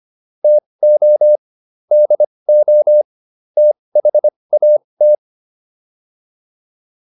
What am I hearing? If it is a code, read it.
Morse code, heard as a keyed tone, about 25 wpm, spaced as for 20 wpm, 600 Hz. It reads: TO DO THAT